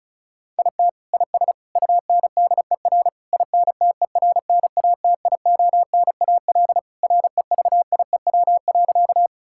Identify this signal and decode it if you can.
Morse code, heard as a keyed tone, about 36 wpm, 700 Hz. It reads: IT IS UNDER INTERNATIONAL REVIEW.
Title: Sentences from Top 1000 Words - Review Part 6 - 35wpm